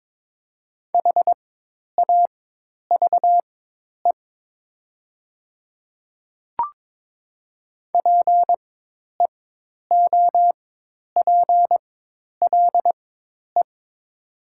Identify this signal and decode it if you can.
Morse code, heard as a keyed tone, about 22 wpm, spaced as for 10 wpm, 700 Hz. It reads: HAVE PEOPLE
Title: 4x Character Spacing - Top 100 Words - 22wpm